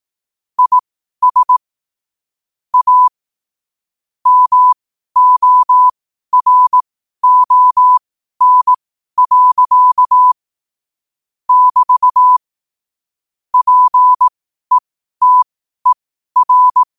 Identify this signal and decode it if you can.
Morse code, heard as a keyed tone, about 18 wpm, spaced as for 12 wpm, 1000 Hz. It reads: IS A MORON. <BT> PETER
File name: qotdfiles / 12wpm